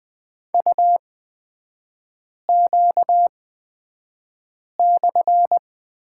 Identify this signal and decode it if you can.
Morse code, heard as a keyed tone, about 20 wpm, spaced as for 5 wpm, 700 Hz. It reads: UQ/
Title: 250 Lesson 40e - BK - Three Characters 20wpm